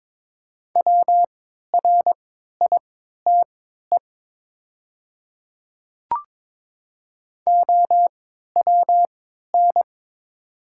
Morse code, heard as a keyed tone, about 22 wpm, spaced as for 13 wpm, 700 Hz. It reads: WRITE OWN
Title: 3x Character Spacing - Top 300 Words - 22wpm